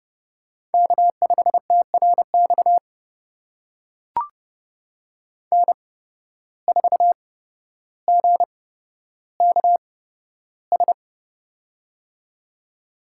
Morse code, heard as a keyed tone, about 32 wpm, spaced as for 13 wpm, 700 Hz. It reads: K5TRX N4GKS
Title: USA Call Signs - 5 Characters - 8x Character Spacing - 30wpm